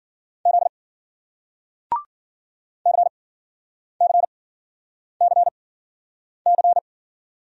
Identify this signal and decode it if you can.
Morse code, heard as a keyed tone, about 56 wpm, spaced as for 9 wpm, 700 Hz. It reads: Z CCCC